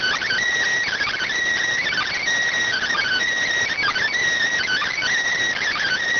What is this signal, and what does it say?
Emergency Alert System (SAME) tones, heard as harsh buzzing bursts. It is heard from 0.9 to 5.0 s.
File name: findings / MR 2 Pel Sag T2 SSFSE 0009.gen-38 rx